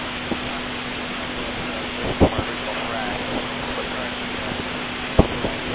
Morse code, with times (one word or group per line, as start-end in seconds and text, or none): none